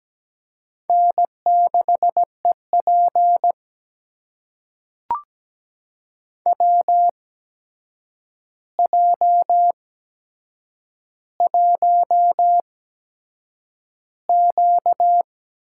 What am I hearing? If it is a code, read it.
Morse code, heard as a keyed tone, about 17 wpm, spaced as for 7 wpm, 700 Hz. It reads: N6EP WJ1Q
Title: USA Call Signs - 8x Character Spacing - 17wpm